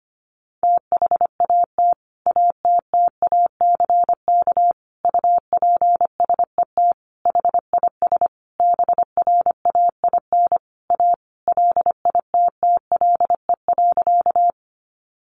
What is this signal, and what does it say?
Morse code, heard as a keyed tone, about 25 wpm, 700 Hz. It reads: THAT ATTACK UPSET HIS BRAIN A LITTLE.
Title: Sentences from Top 1500 Words 25wpm